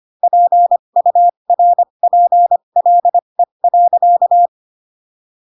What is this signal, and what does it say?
Morse code, heard as a keyed tone, about 25 wpm, spaced as for 22 wpm, 700 Hz. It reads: PURPLE.